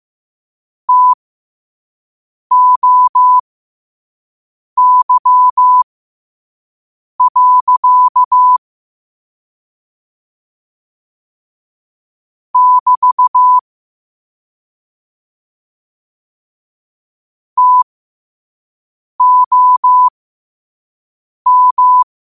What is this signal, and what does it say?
Morse code, heard as a keyed tone, about 15 wpm, spaced as for 5 wpm, 1000 Hz. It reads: TOY. <BT> TOM
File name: qotdfiles / 5wpm